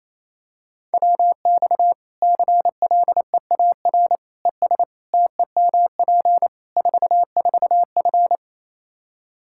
Morse code, heard as a keyed tone, about 28 wpm, 700 Hz. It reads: WX CLEAR ES TEMP 44F
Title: Practice QSO - Exchange 2 - Info - 28wpm